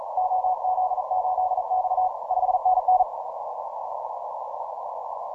Morse code, heard as a keyed tone, about 38 wpm, 740 Hz. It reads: OZ8X 5NN
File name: simple-file-list / C92RU 12MCW